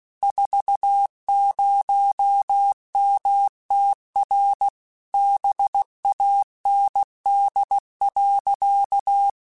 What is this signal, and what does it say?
Morse code, heard as a keyed tone, about 16 wpm, 800 Hz. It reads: 40MTR BAND.
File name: QSOs / QSOs-mp3s / kc7haa@15